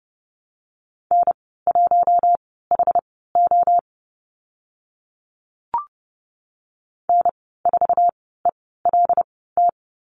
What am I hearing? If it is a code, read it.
Morse code, heard as a keyed tone, about 30 wpm, spaced as for 17 wpm, 700 Hz. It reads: N1HO N4ELT
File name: Lesson 15g - Number 5 - Call Signs 30wpm